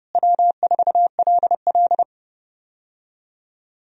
Morse code, heard as a keyed tone, about 30 wpm, 700 Hz. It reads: W4LL
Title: USA Call Signs - 4 Characters - Repeated 3x - 30wpm